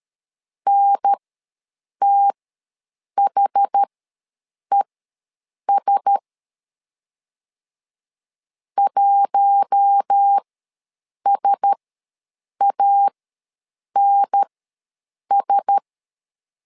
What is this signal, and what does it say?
Morse code, heard as a keyed tone, about 13 wpm, spaced as for 7 wpm, 800 Hz. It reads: NTHES 1SANS
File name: LEZIONI / 4 Q1S ws 12wpm 1827